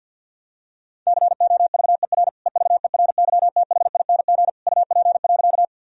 Morse code, heard as a keyed tone, about 52 wpm, 700 Hz. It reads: COVER EVERYTHING UP.
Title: Sentences from Top 1000 Words - Review Part 1 - 50wpm